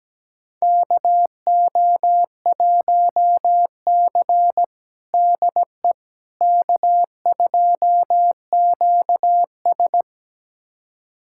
Morse code, heard as a keyed tone, about 17 wpm, 700 Hz. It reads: KO1C DE K2QS
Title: USA Callsign DE Callsign - 4 Characters - 17wpm